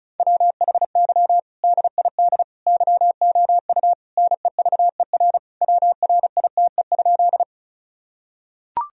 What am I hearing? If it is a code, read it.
Morse code, heard as a keyed tone, about 36 wpm, 700 Hz. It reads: WHY DID YOU NEVER WRITE?
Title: Sentences from Top 1000 Words - Review Part 1 - 35wpm